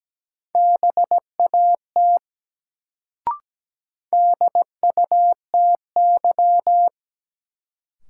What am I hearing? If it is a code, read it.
Morse code, heard as a keyed tone, about 17 wpm, 700 Hz. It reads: BAT DUTY
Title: Sentences from Top 1500 Words - New Word Drill - 17wpm